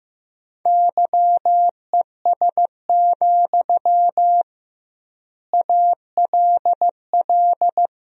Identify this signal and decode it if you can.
Morse code, heard as a keyed tone, about 15 wpm, 700 Hz. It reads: YES, ALL
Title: 2x Word-Spacing - Sentences from Top 500 Words - 15wpm